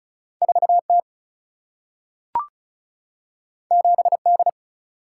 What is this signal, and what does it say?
Morse code, heard as a keyed tone, about 36 wpm, 700 Hz. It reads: 4 T 7 D
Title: Single-Letter Single-Number 35wpm